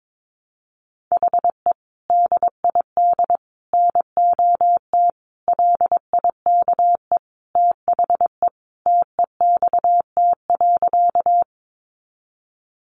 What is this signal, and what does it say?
Morse code, heard as a keyed tone, about 22 wpm, 700 Hz. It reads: HE DID NOT LIKE THE TEXT.
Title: Sentences from Top 1200 Words 22wpm